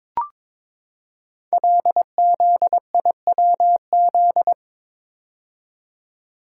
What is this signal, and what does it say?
Morse code, heard as a keyed tone, about 22 wpm, 700 Hz. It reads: LZIWZ
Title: Code Groups - 5 Letters - 22wpm